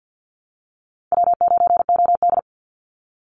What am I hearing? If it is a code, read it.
Morse code, heard as a keyed tone, about 50 wpm, 700 Hz. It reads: W9OD
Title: Easy Call Signs - 50wpm